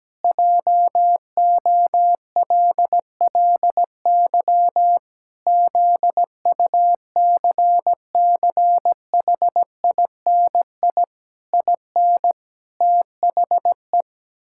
Morse code, heard as a keyed tone, about 17 wpm, 700 Hz. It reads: JOLLY ZUCCHINI IN THE
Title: Pangram Challenge v2 - 17wpm